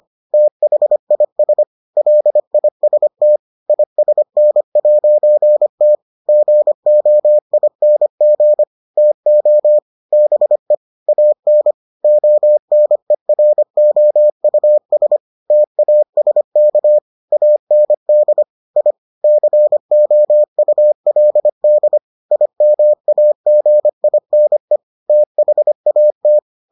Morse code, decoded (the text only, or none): THIS LIST ISN'T GOING TO BE AN ONEROUS TASK AND I COULD IMAGINE THAT